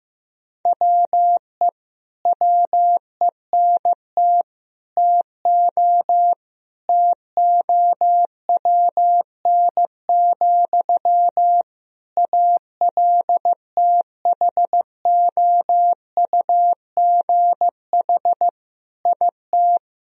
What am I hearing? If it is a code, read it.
Morse code, heard as a keyed tone, about 15 wpm, 700 Hz. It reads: WE WENT TO TOWN, ALTHOUGH IT
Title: Sentences from Top 1000 Words - Review Part 3 - 15wpm